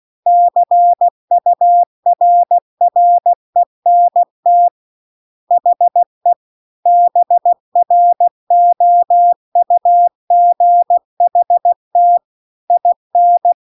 Morse code, heard as a keyed tone, about 16 wpm, 700 Hz. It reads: CURRENT HE BROUGHT IN
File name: CW-The-Big-Twoheaded-River-2-16WPM